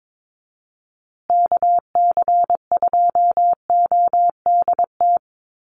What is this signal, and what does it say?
Morse code, heard as a keyed tone, about 22 wpm, 700 Hz. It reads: KC2ODT
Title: USA Call Signs Repeated 2x 22wpm